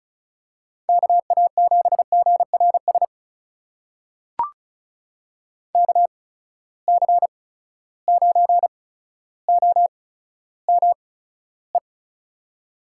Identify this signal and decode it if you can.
Morse code, heard as a keyed tone, about 36 wpm, spaced as for 15 wpm, 700 Hz. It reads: KA7GRS KC9OME
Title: USA Call Signs - 6 Characters - 8x Character Spacing - 35wpm